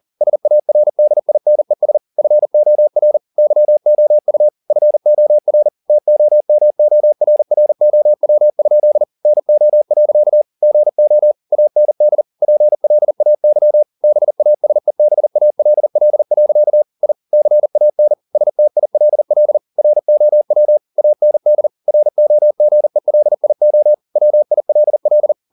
Morse code, heard as a keyed tone, about 40 wpm, 600 Hz. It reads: SARDINES FOR YOU FOR TOMORROW? NO. GO AND PLAY BASEBALL. I CAN STILL ROW AND ROGELIO WILL
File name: the old man and the sea 1 40wpm